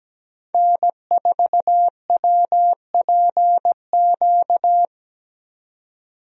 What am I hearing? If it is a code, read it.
Morse code, heard as a keyed tone, about 17 wpm, 700 Hz. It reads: N4WPQ
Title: USA Call Signs - 5 Characters - 17wpm